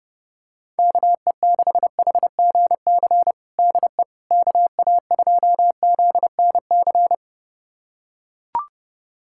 Morse code, heard as a keyed tone, about 32 wpm, 690 Hz. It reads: KE6HGC DE KA2ZNC
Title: USA Callsign DE Callsign - 6 Characters - 30wpm